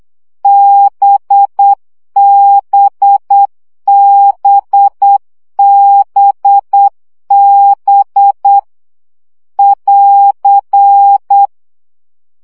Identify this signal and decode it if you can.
Morse code, heard as a keyed tone, about 8 wpm, 800 Hz. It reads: BBBBB <AR>
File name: CW gyakorlas b 7 baud